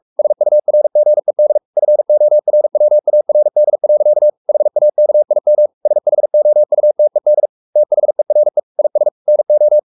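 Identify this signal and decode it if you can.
Morse code, heard as a keyed tone, about 44 wpm, 600 Hz. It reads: SURGED FORWARD. HAKIM SHOUTED THERE IS NO
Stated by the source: Morse code at 43 wpm sending Ken Follett's novel Never in English, from CW med Gnister.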